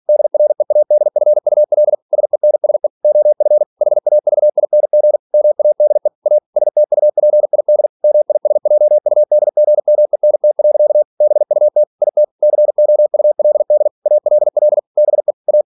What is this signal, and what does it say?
Morse code, heard as a keyed tone, about 46 wpm, 600 Hz. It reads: DREADFUL SENSE OF HAVING MADE A STUPID MISJUDGMENT. BUT IT COULD ALL BE A
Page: Never 47wpm